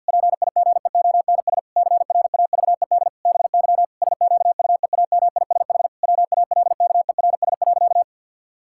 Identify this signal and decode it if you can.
Morse code, heard as a keyed tone, about 50 wpm, 700 Hz. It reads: PIGEONS CRAVED BY SQUEAMISH WALKERS.
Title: Pangram Challenge v2 - 50wpm